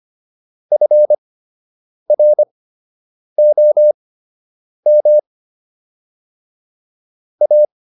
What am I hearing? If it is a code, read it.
Morse code, heard as a keyed tone, about 25 wpm, spaced as for 8 wpm, 600 Hz. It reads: FROM A